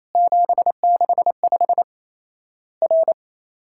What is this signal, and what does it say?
Morse code, heard as a keyed tone, about 28 wpm, 700 Hz. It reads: 765 R
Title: Worked All Europe DX Contest - QTC Exchange - 28wpm